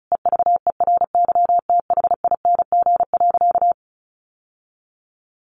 Morse code, heard as a keyed tone, about 34 wpm, 700 Hz. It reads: EVERYTHING.
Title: Sentences from Top 900 Words 35wpm